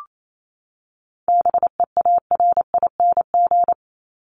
Morse code, heard as a keyed tone, about 28 wpm, 700 Hz. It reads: BEARING
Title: ING Suffix - Top 100 Words with Base Words 28wpm